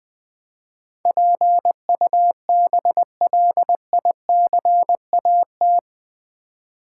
Morse code, heard as a keyed tone, about 20 wpm, 700 Hz. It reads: PUBLICAT
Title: Build A Word - 4 Letters At A Time - 11-Letter Words - 20wpm